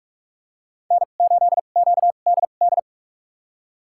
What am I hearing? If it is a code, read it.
Morse code, heard as a keyed tone, about 46 wpm, spaced as for 32 wpm, 700 Hz. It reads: N8QDD
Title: USA Call Signs - 2x Character Spacing - 45wpm